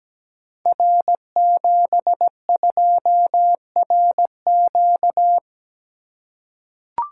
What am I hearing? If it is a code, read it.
Morse code, heard as a keyed tone, about 17 wpm, 700 Hz. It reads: R72RQ
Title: Code Groups - 5 Letter:Numbers - 17wpm